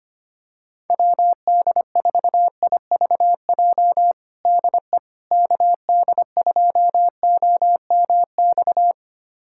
Morse code, heard as a keyed tone, about 25 wpm, 700 Hz. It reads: WD4IVJ DE KD2OMX
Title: USA Callsign DE Callsign 25wpm